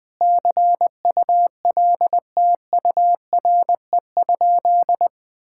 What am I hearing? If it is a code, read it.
Morse code, heard as a keyed tone, about 21 wpm, 700 Hz. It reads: CULTURE?
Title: Sentences from Top 1000 Words - Review Part 6 - 20wpm